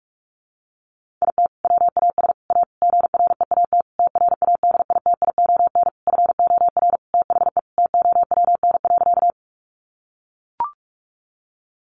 Morse code, heard as a keyed tone, about 44 wpm, 700 Hz. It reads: IT WAS A GREAT TRADITION FOR THE TOWN.
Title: Sentences from Top 1500 Words 45wpm